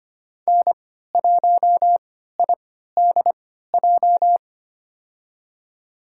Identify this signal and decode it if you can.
Morse code, heard as a keyed tone, about 25 wpm, spaced as for 14 wpm, 700 Hz. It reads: N1IDJ
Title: USA Call Signs - 5 Characters - 3x Character Spacing - 25wpm